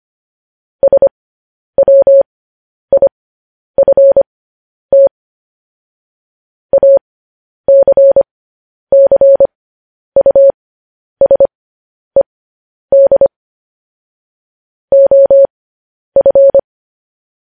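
Morse code, heard as a keyed tone, about 25 wpm, spaced as for 10 wpm, 550 Hz. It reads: SWIFT ACCUSED OF
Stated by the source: Morse code news headlines broadcast, opening with vvv.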